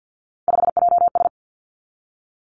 Morse code, heard as a keyed tone, about 50 wpm, 700 Hz. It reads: 5JS
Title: Three Challenging Letter-Number Combinations 50wpm